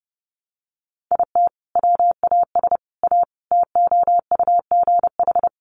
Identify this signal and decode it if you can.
Morse code, heard as a keyed tone, about 30 wpm, 700 Hz. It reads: IT WAS A TOUGH
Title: Sentences from Top 1200 Words 30wpm